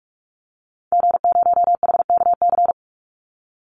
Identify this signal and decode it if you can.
Morse code, heard as a keyed tone, about 44 wpm, 700 Hz. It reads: G0HKC
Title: Call Signs Repeated 2x - 45wpm